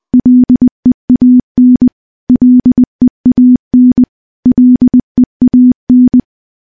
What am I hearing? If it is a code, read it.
Morse code, heard as a keyed tone, about 20 wpm, 250 Hz. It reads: LEAN LEAN LEAN
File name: morse / morse-training-20WPM